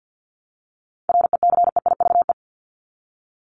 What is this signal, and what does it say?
Morse code, heard as a keyed tone, about 50 wpm, 700 Hz. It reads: RECEIVE
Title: ED Suffix - Top 100 Words with Base Words 50wpm